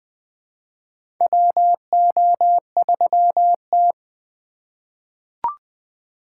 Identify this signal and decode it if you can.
Morse code, heard as a keyed tone, about 20 wpm, 700 Hz. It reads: WO3T
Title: USA Call Signs - 4 Characters - 2x Character Spacing - 20wpm